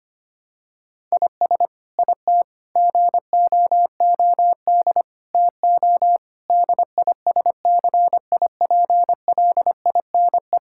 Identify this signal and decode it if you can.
Morse code, heard as a keyed tone, about 25 wpm, 700 Hz. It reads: IS IT GOOD TO DISCIPLINE
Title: Sentences from Top 1000 Words - Review Part 6 - 25wpm